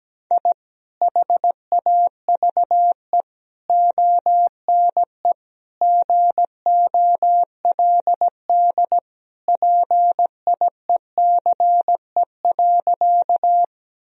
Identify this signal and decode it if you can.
Morse code, heard as a keyed tone, about 17 wpm, 700 Hz. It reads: I HAVE ONE GOLD PIECE.
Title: Sentences from Top 1000 Words - Review Part 3 - 17wpm